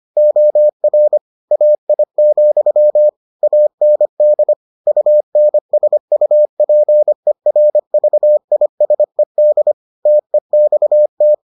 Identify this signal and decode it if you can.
Morse code, heard as a keyed tone, about 25 wpm, 600 Hz. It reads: OR AI, AND UNSUPERVISED TEXT